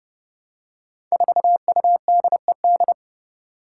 Morse code, heard as a keyed tone, about 30 wpm, 700 Hz. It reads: 4UDED
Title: Five Challenging Letter-Number Combinations 30wpm